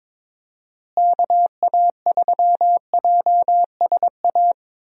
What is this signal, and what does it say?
Morse code, heard as a keyed tone, about 22 wpm, 700 Hz. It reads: KA3JSA
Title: USA Call Signs - 6 Characters - Repeated 3x - 22wpm